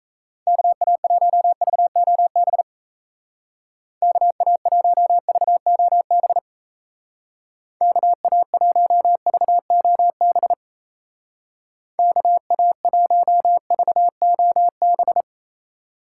Morse code, heard as a keyed tone, about 34 wpm, 700 Hz. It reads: KA1VOB KA1VOB KA1VOB KA1VOB
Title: Speed Racing - USA 2x3 Call Signs 28wpm